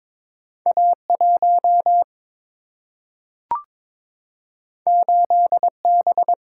Morse code, heard as a keyed tone, about 22 wpm, 700 Hz. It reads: A 1 8 B
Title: Single-Letter Single-Number 22wpm